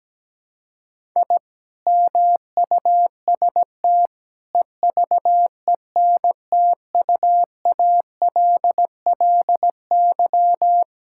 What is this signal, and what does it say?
Morse code, heard as a keyed tone, about 17 wpm, 700 Hz. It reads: I MUST EVENTUALLY
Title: Sentences from Top 1000 Words - Review Part 6 - 17wpm